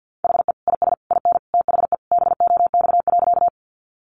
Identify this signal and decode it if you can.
Morse code, heard as a keyed tone, about 50 wpm, 700 Hz. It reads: HE IS IN THE BOX.
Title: Sentences from Top 500 Words 50wpm